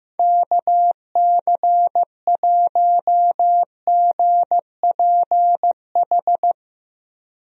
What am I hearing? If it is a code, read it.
Morse code, heard as a keyed tone, about 15 wpm, 700 Hz. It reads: KC1GPH
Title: USA Call Signs - 6 Characters - Repeated 3x - 15wpm